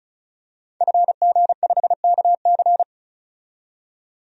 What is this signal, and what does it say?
Morse code, heard as a keyed tone, about 34 wpm, 700 Hz. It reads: FG5KC
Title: DXCC-100 Call Signs - 5 Characters - 35wpm